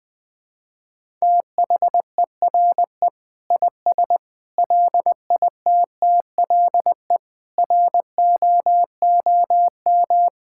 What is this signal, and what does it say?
Morse code, heard as a keyed tone, about 21 wpm, 700 Hz. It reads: THERE IS LITTLE ROOM
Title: Sentences from Top 1000 Words - Review Part 6 - 20wpm